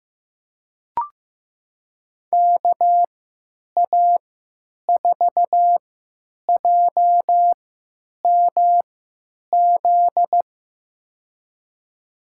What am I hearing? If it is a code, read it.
Morse code, heard as a keyed tone, about 15 wpm, spaced as for 8 wpm, 700 Hz. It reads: KA4JMZ
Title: USA Call Signs - 6 Characters - 3x Character Spacing - 15wpm